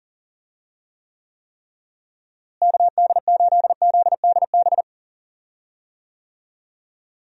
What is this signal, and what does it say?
Morse code, heard as a keyed tone, about 40 wpm, 700 Hz. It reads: KD8ZDB
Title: USA Call Signs - 6 Characters - Repeated 3x - 40wpm